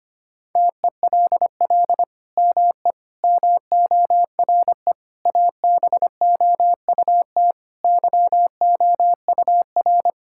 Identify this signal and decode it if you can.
Morse code, heard as a keyed tone, about 25 wpm, 700 Hz. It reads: TELL ME MORE ABOUT YOUR
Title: Sentences from Top 1000 Words - Review Part 5 - 25wpm